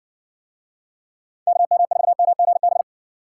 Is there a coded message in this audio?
Yes: Morse code from 1.5 to 2.8 s, heard as a keyed tone, about 54 wpm, 700 Hz. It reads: KM4MGB